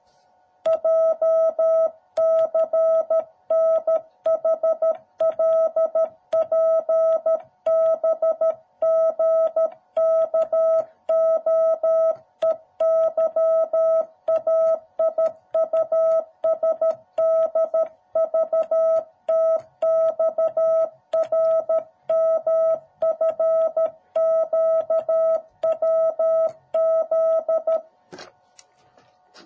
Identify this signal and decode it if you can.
Morse code, heard as a keyed tone, about 13 wpm, 650 Hz. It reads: JCNHLPBGKOEYAIUSDVTXRMFQWZ